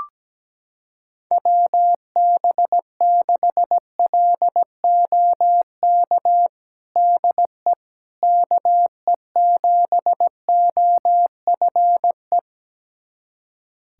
Morse code, heard as a keyed tone, about 17 wpm, 700 Hz. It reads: WB6LOK DE KE7OFE
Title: USA Callsign DE Callsign - 6 Characters - 17wpm